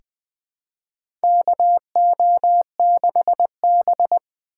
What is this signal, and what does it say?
Morse code, heard as a keyed tone, about 20 wpm, 700 Hz. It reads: KO6B
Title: USA Call Signs - 3x Character Spacing - 20wpm